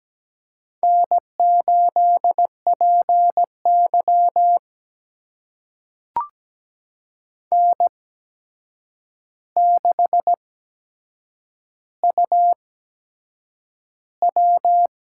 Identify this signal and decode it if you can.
Morse code, heard as a keyed tone, about 17 wpm, spaced as for 7 wpm, 700 Hz. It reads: N8PY N6UW
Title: USA Call Signs - 4 Characters - 8x Character Spacing - 17wpm